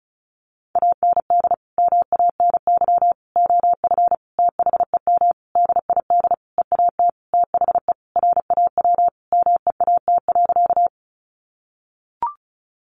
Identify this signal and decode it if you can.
Morse code, heard as a keyed tone, about 34 wpm, 700 Hz. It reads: AND MANY OF THEM DID EAT THE RAW MEAT.
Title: Sentences from Top 1500 Words 35wpm